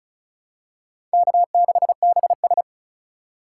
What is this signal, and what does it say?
Morse code, heard as a keyed tone, about 36 wpm, 700 Hz. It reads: K6BS
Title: USA Call Signs - 4 Characters - 6x Character Spacing - 35wpm